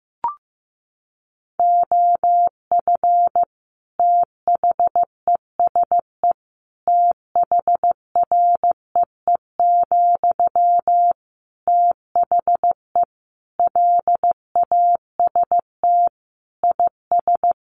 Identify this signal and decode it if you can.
Morse code, heard as a keyed tone, about 15 wpm, 700 Hz. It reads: OF THESE THREE, THE LAST IS
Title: Sentences from Top 700 Words 15wpm